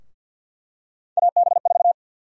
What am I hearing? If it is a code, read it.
Morse code, heard as a keyed tone, about 49 wpm, 700 Hz. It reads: AB4
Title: USA Call Sign Prefixes - 50wpm